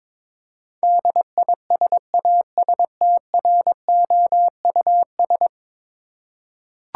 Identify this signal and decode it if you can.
Morse code, heard as a keyed tone, about 22 wpm, 700 Hz. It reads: DISASTROUS
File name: practice / 22wpm / 10 Letter Words - First 3 Letters Given - 22wpm